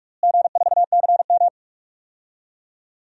Morse code, heard as a keyed tone, about 42 wpm, 700 Hz. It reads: G4CM
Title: DXCC-100 Call Signs - 4 Characters - Repeated 3x - 45wpm